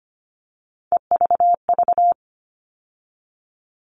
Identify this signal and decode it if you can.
Morse code, heard as a keyed tone, about 25 wpm, 700 Hz. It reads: EVV
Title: All Cut Numbers - 3-Digit 25wpm